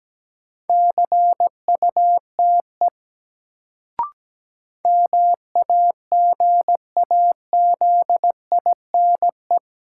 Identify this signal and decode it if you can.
Morse code, heard as a keyed tone, about 17 wpm, 700 Hz. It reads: CUTE MAGAZINE
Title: Sentences from Top 1000 Words - Review Part 7 - New Word Drill - 17wpm